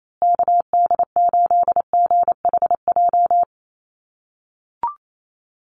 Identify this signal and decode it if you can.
Morse code, heard as a keyed tone, about 28 wpm, 700 Hz. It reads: KD8GHJ
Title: USA Call Signs Repeated 2x 28wpm